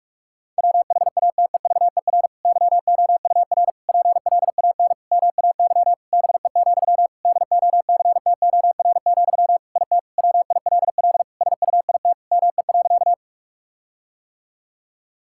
Morse code, heard as a keyed tone, about 46 wpm, 700 Hz. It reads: WHATEVER YOUR PLAN MAY BE, DOCTOR, IT WILL SUIT ME.
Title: Sentences from Top 1600 Words 45wpm